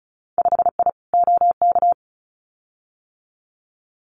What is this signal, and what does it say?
Morse code, heard as a keyed tone, about 34 wpm, 700 Hz. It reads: 5I OK
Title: Winter Field Day Exchanges Encore 35wpm